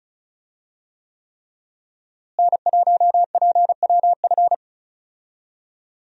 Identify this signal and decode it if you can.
Morse code, heard as a keyed tone, about 35 wpm, 700 Hz. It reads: N1PWF
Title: USA Call Signs - 5 Characters - Repeated 3x - 35wpm